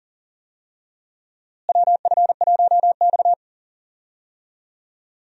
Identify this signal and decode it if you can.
Morse code, heard as a keyed tone, about 39 wpm, 700 Hz. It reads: WF1X
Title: USA Call Signs - 4 Characters - Repeated 2x - 40wpm